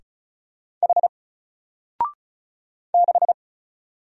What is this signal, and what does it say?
Morse code, heard as a keyed tone, about 36 wpm, 700 Hz. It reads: H 6
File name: Single-Letter-Number-35wpm